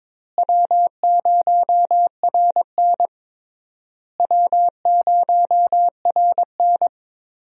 Morse code, heard as a keyed tone, about 22 wpm, 700 Hz. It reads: W0RN W0RN
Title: USA Call Signs - 4 Characters - Repeated 3x - 22wpm